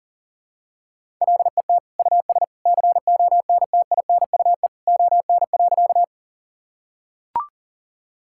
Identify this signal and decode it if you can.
Morse code, heard as a keyed tone, about 37 wpm, 700 Hz. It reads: LET US CONTINUE ON.
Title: Sentences from Top 1000 Words - Review Part 6 - 40wpm